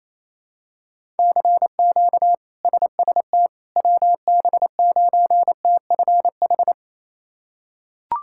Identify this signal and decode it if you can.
Morse code, heard as a keyed tone, about 28 wpm, 700 Hz. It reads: CQ SST WB9TFH
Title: K1USN SST QSO Practice - 28wpm